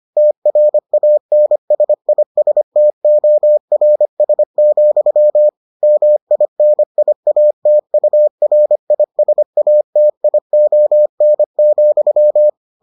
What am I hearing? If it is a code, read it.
Morse code, heard as a keyed tone, about 25 wpm, 600 Hz. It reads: TRANSISTORS, MINIATURISATION,